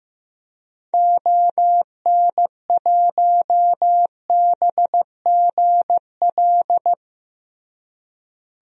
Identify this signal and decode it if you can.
Morse code, heard as a keyed tone, about 15 wpm, 700 Hz. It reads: ON1BGL
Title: DXCC-100 Call Signs - 6 Characters - Repeated 2x - 15wpm